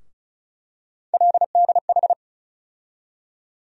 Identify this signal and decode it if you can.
Morse code, heard as a keyed tone, about 36 wpm, 700 Hz. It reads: LDH
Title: USA Call Sign Postfixes - 3 Letters 35wpm